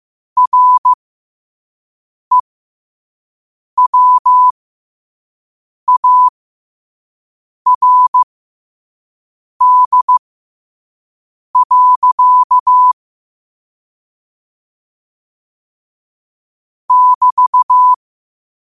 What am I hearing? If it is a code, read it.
Morse code, heard as a keyed tone, about 15 wpm, spaced as for 5 wpm, 1000 Hz. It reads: REWARD. <BT>
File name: qotdfiles / 5wpm